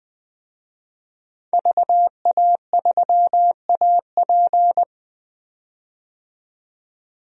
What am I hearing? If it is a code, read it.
Morse code, heard as a keyed tone, about 20 wpm, 700 Hz. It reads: VA3AP
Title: DXCC-100 Call Signs - 5 Characters - 20wpm